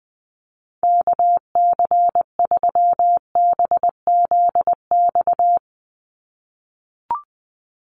Morse code, heard as a keyed tone, about 20 wpm, 700 Hz. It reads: KC3BZX
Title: USA Call Signs 20wpm